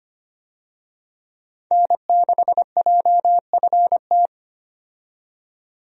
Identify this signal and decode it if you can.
Morse code, heard as a keyed tone, about 25 wpm, 700 Hz. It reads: N6JFT
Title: USA Call Signs - 5 Characters - 25wpm